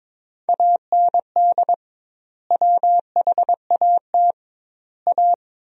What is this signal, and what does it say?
Morse code, heard as a keyed tone, about 22 wpm, 700 Hz. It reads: AND WHAT A
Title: 2x Word-Spacing - Sentences from Top 500 Words - 22wpm